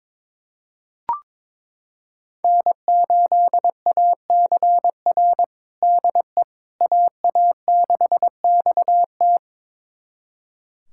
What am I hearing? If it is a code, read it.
Morse code, heard as a keyed tone, about 22 wpm, 700 Hz. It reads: N8ACR DE AA6XT
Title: USA Callsign DE Callsign 22wpm